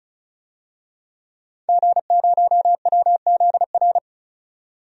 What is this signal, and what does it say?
Morse code, heard as a keyed tone, about 36 wpm, 700 Hz. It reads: G0WZR
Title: DXCC-100 Call Signs - 5 Characters - 35wpm